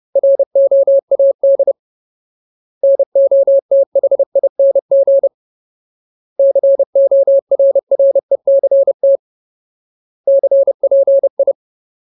Morse code, heard as a keyed tone, about 30 wpm, 550 Hz. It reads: ROAD NOTHING CORRECT CPI
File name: Ultimate Word List / 30WPM Ultimate